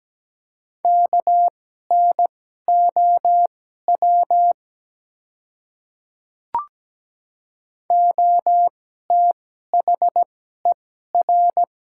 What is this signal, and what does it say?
Morse code, heard as a keyed tone, about 17 wpm, spaced as for 12 wpm, 700 Hz. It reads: KNOW OTHER
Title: 2x Character Spacing - Top 300 Words - 17wpm